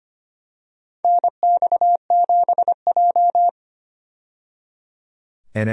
Morse code, heard as a keyed tone, about 25 wpm, 700 Hz. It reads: NX7J
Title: USA Call Signs - 4 Characters - 25wpm